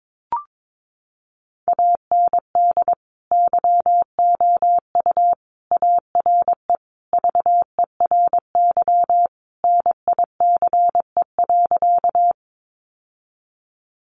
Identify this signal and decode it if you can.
Morse code, heard as a keyed tone, about 22 wpm, 700 Hz. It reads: AND YOU ARE VERY NICE.
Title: Sentences from Top 1100 Words 22wpm